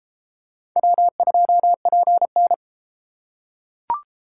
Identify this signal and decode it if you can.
Morse code, heard as a keyed tone, about 32 wpm, 700 Hz. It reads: W2PN